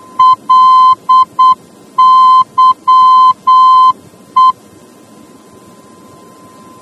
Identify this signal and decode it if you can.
Morse code, heard as a keyed tone, about 8 wpm, 1030 Hz. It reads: LYE